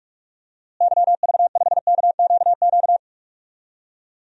Morse code, heard as a keyed tone, about 46 wpm, 700 Hz. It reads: YV5KQQ
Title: DXCC-100 Call Signs - 6 Characters - Repeated 3x - 45wpm